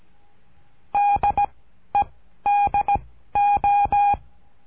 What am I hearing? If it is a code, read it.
Morse code, heard as a keyed tone, about 17 wpm, 820 Hz. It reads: D E D O
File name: cursos / CURSO DE TELEGRAFIA / Telegrafia - Trilha 13